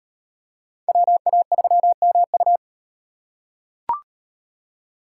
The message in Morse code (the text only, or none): WA3MU